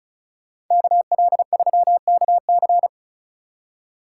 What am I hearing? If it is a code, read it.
Morse code, heard as a keyed tone, about 47 wpm, 700 Hz. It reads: KL3KC